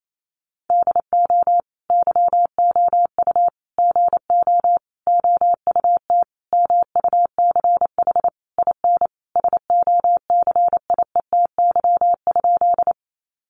Morse code, heard as a keyed tone, about 28 wpm, 700 Hz. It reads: DO YOU GO OUT MUCH IN SOCIETY?
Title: Sentences from Top 700 Words 28wpm